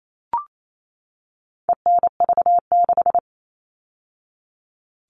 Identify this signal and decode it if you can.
Morse code, heard as a keyed tone, about 28 wpm, 700 Hz. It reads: ENV6
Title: All Cut Numbers - 4-Digit 28wpm